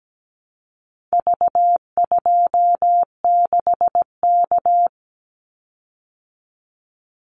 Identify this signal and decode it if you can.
Morse code, heard as a keyed tone, about 17 wpm, 700 Hz. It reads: V26K
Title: Call Signs Repeated 2x - 17wpm